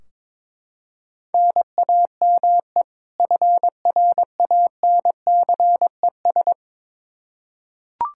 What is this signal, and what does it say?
Morse code, heard as a keyed tone, about 21 wpm, 700 Hz. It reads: NAME FRANCES
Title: Practice QSO - Exchange 1 - Beginning - 22wpm